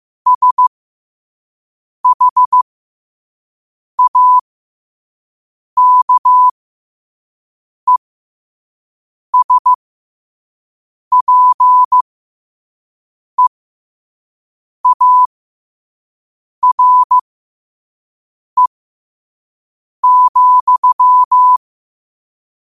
Morse code, heard as a keyed tone, about 15 wpm, spaced as for 5 wpm, 1000 Hz. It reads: SHAKESPEARE,